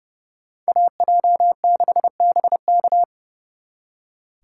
Morse code, heard as a keyed tone, about 36 wpm, 700 Hz. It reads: AJ6BK
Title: Speed Racing - USA 2x2 Call Signs 30wpm